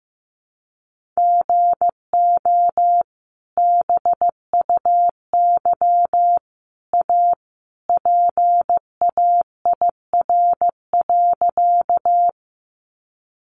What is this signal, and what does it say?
Morse code, heard as a keyed tone, about 15 wpm, 700 Hz. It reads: GO BUY A PAIR.
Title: Sentences from Top 1200 Words 15wpm